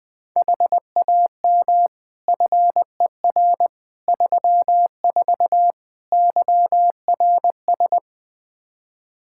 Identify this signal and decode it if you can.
Morse code, heard as a keyed tone, about 20 wpm, 700 Hz. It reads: HAM FER 34 YRS
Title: Practice QSO - Exchange 3 - More Info - 20wpm